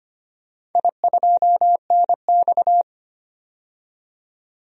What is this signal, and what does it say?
Morse code, heard as a keyed tone, about 25 wpm, 700 Hz. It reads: I2NX
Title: DXCC-100 Call Signs - 4 Characters - 25wpm